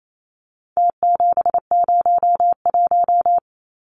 Morse code, heard as a keyed tone, about 28 wpm, 700 Hz. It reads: T701
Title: Numbers - Contest Style - 4-Digit 28wpm